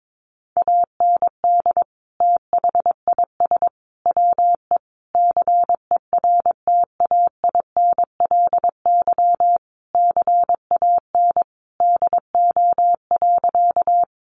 Morse code, heard as a keyed tone, about 22 wpm, 700 Hz. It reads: AND THIS WE CERTAINLY CAN DO.
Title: Sentences from Top 900 Words 22wpm